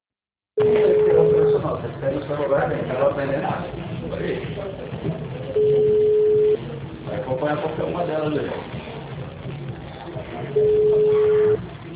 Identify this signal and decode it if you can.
Ringback tone (Europe, 425 Hz), with speech.